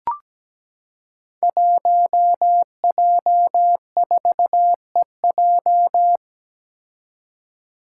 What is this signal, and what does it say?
Morse code, heard as a keyed tone, about 17 wpm, 700 Hz. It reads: 1J4EJ
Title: Five Challenging Letter-Number Combinations 17wpm